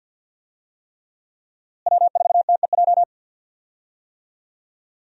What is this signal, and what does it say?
Morse code, heard as a keyed tone, about 49 wpm, 700 Hz. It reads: W4TEJ
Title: USA Call Signs - 5 Characters - Repeated 3x - 50wpm